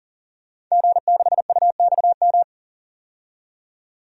Morse code, heard as a keyed tone, about 40 wpm, 700 Hz. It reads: G6UXM
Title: DXCC-100 Call Signs - 5 Characters - 40wpm